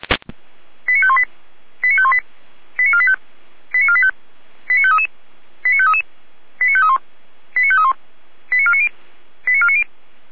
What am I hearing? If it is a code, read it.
Five-tone selective call (ZVEI), heard as quick runs of five tones, tones of about 70 ms. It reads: 89518 89518 89485 89485 89533 89533 89531 89531 89490 89490
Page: SDR/ZVEI